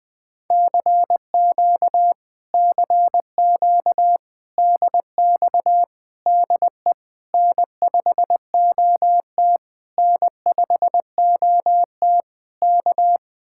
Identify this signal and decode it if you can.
Morse code, heard as a keyed tone, about 20 wpm, 700 Hz. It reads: CQ CQ DX DE N5OT N5OT K
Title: Speed-Racing Common QSO Phrases - 17wpm